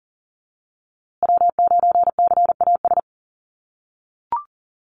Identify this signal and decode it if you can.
Morse code, heard as a keyed tone, about 39 wpm, 700 Hz. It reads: W9CAS
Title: USA Call Signs Repeated 2x 40wpm